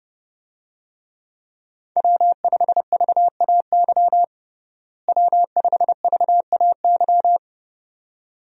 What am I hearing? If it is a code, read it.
Morse code, heard as a keyed tone, about 32 wpm, 700 Hz. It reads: W5VAY W5VAY
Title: USA Call Signs - 5 Characters - Repeated 2x - 30wpm